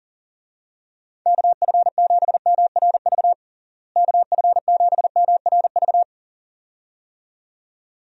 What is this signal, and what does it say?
Morse code, heard as a keyed tone, about 37 wpm, 700 Hz. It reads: KF7MRV KF7MRV
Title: USA Call Signs - 6 Characters - Repeated 2x - 40wpm